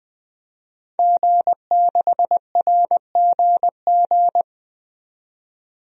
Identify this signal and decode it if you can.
Morse code, heard as a keyed tone, about 20 wpm, 700 Hz. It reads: G6RGG
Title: DXCC-100 Call Signs - 5 Characters - 20wpm